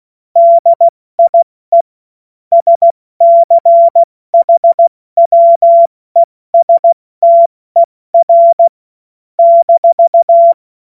Morse code, heard as a keyed tone, about 16 wpm, 680 Hz. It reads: DIE SCHWESTER -